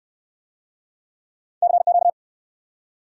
Morse code, heard as a keyed tone, about 68 wpm, 700 Hz. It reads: CQ